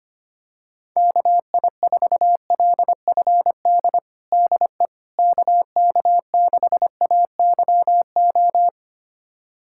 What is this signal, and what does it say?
Morse code, heard as a keyed tone, about 25 wpm, 700 Hz. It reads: KI4LFD DE KK6AYO
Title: USA Callsign DE Callsign 25wpm